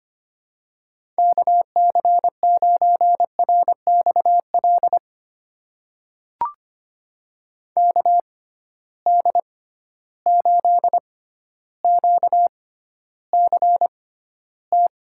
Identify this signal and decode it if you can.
Morse code, heard as a keyed tone, about 25 wpm, spaced as for 13 wpm, 700 Hz. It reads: KC9RXL KD8QCT
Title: USA Call Signs - 6 Characters - 6x Character Spacing - 25wpm